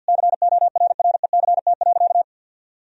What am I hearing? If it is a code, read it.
Morse code, heard as a keyed tone, about 50 wpm, 700 Hz. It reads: CORRECT.
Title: Sentences from Top 1000 Words - Review Part 3 - 50wpm